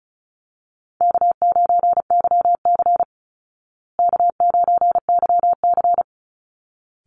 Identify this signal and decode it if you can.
Morse code, heard as a keyed tone, about 34 wpm, 700 Hz. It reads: K9YC K9YC
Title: Easy Call Signs Repeated 2x - 35wpm